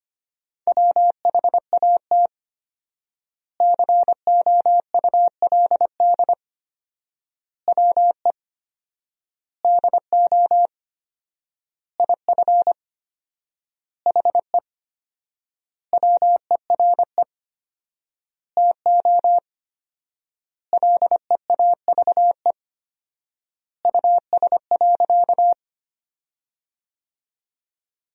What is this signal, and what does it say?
Morse code, heard as a keyed tone, about 25 wpm, 700 Hz. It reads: WHAT COULD WE DO IF HE WERE TO LEAVE US.
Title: 4x Word-Spacing - Sentences from Top 400 Words - 25wpm